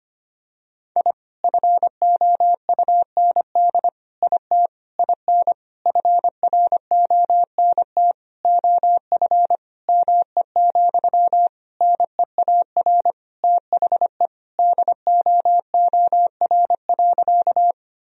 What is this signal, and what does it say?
Morse code, heard as a keyed tone, about 25 wpm, 700 Hz. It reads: I FOUND IT IN FRONT OF ME, NEAR THE DOOR.
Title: Sentences from Top 1000 Words - Review Part 2 - 25wpm